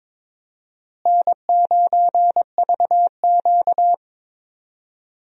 Morse code, heard as a keyed tone, about 22 wpm, 700 Hz. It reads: N9VQ
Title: USA Call Signs - 4 Characters - Repeated 3x - 22wpm